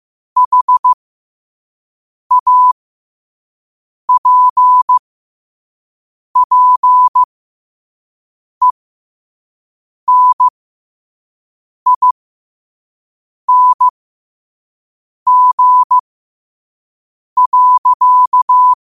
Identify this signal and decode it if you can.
Morse code, heard as a keyed tone, about 15 wpm, spaced as for 5 wpm, 1000 Hz. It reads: HAPPENING.
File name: qotdfiles / 5wpm